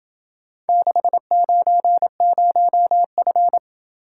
Morse code, heard as a keyed tone, about 27 wpm, 700 Hz. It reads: 690F